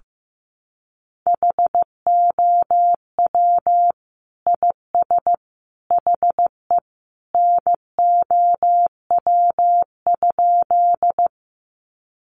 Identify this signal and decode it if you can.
Morse code, heard as a keyed tone, about 15 wpm, 700 Hz. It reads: HOW IS HE NOW?
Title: Sentences from Top 100 Words 15wpm